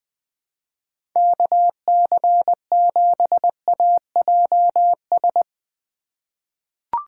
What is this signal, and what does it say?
Morse code, heard as a keyed tone, about 21 wpm, 700 Hz. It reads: KC7AJS
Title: USA Call Signs - 3x Character Spacing - 20wpm